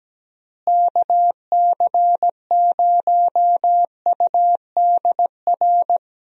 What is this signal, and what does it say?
Morse code, heard as a keyed tone, about 17 wpm, 700 Hz. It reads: KC0UDR
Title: USA Call Signs - 6 Characters - Repeated 2x - 17wpm